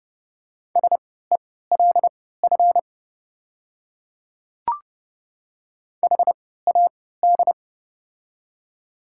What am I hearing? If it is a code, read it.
Morse code, heard as a keyed tone, about 32 wpm, spaced as for 17 wpm, 690 Hz. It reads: SELF HAD
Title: 3x Character Spacing - Top 300 Words - 30wpm